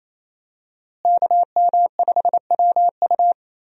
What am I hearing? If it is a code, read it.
Morse code, heard as a keyed tone, about 28 wpm, 700 Hz. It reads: KM5WU
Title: USA Call Signs - 5 Characters - Repeated 2x - 28wpm